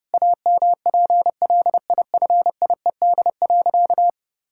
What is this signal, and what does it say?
Morse code, heard as a keyed tone, about 30 wpm, 700 Hz. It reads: AMPLIFIED.
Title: Pangram Challenge v2 - 30wpm